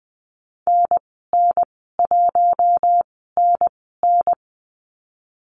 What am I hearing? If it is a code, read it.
Morse code, heard as a keyed tone, about 20 wpm, spaced as for 14 wpm, 700 Hz. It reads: NN1NN
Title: 076 Lesson 13g - Letter L - Call Signs 20wpm